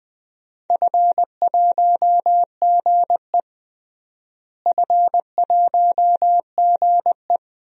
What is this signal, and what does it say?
Morse code, heard as a keyed tone, about 20 wpm, 700 Hz. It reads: F1GE F1GE
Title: DXCC-100 Call Signs - 4 Characters - Repeated 3x - 20wpm